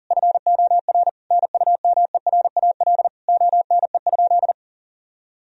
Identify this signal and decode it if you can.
Morse code, heard as a keyed tone, about 40 wpm, 700 Hz. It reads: FOR NUMERAL ONE?
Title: Sentences from Top 1000 Words - Review Part 2 - 40wpm